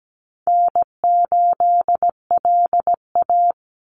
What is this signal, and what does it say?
Morse code, heard as a keyed tone, about 17 wpm, 700 Hz. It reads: N8LA
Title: Easy Call Signs Repeated 3x - 17wpm